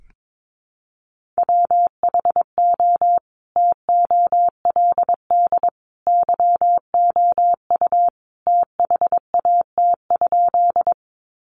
Morse code, heard as a keyed tone, about 22 wpm, 700 Hz. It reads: WHO TOLD YOU THAT?
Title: Sentences from Top 400 Words 22wpm